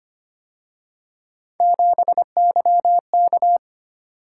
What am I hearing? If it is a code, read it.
Morse code, heard as a keyed tone, about 29 wpm, 700 Hz. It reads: 7YK
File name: Speed-Racing - Three Letter-Number Combinations - 25wpm